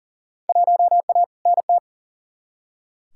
Morse code, heard as a keyed tone, about 38 wpm, 700 Hz. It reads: 1A NT
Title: Field Day Exchanges Encore - 40wpm